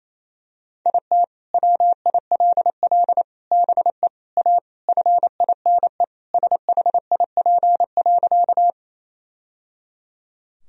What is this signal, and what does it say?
Morse code, heard as a keyed tone, about 28 wpm, 700 Hz. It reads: IT WILL BE A FINE SHIP.
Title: Sentences from Top 1000 Words - Review Part 2 - 28wpm